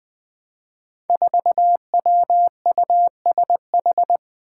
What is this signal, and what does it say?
Morse code, heard as a keyed tone, about 20 wpm, 700 Hz. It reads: 4WUSH
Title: Five Challenging Letter-Number Combinations 20wpm